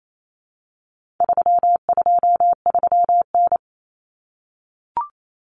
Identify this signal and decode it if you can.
Morse code, heard as a keyed tone, about 28 wpm, 700 Hz. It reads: 323N
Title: Common Cut Numbers - 4-Digit 28wpm